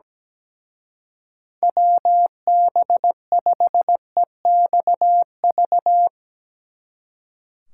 Morse code, heard as a keyed tone, about 18 wpm, 700 Hz. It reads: WB5EXV
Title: Speed Racing - USA 2x3 Call Signs 17wpm